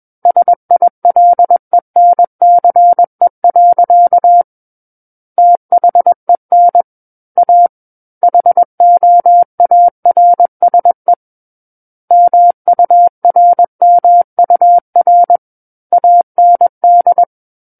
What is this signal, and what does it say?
Morse code, heard as a keyed tone, about 21 wpm, 700 Hz. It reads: SILENCE. THEN A HOARSE MURMUR AND
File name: Book One / 12-Chapter 12 - 21 WPM